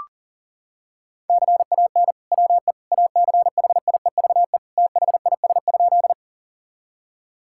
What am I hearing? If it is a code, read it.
Morse code, heard as a keyed tone, about 40 wpm, 700 Hz. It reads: CAN WE ACHIEVE THIS?
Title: Sentences from Top 1000 Words - Review Part 6 - 40wpm